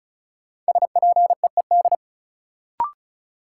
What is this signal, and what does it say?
Morse code, heard as a keyed tone, about 36 wpm, 700 Hz. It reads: SPEED
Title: Sentences from Top 800 Words - New Word Drill - 35wpm